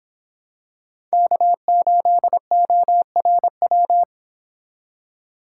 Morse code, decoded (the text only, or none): K8ORW